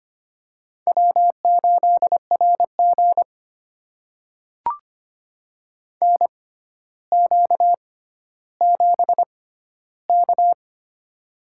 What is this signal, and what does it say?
Morse code, heard as a keyed tone, about 25 wpm, spaced as for 13 wpm, 700 Hz. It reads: W8RG NQ7K
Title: USA Call Signs - 6x Character Spacing - 25wpm